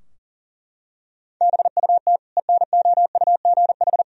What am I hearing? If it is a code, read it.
Morse code, heard as a keyed tone, about 40 wpm, 700 Hz. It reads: BUT ENOUGH
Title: Sentences from Top 1000 Words - Review Part 2 - 40wpm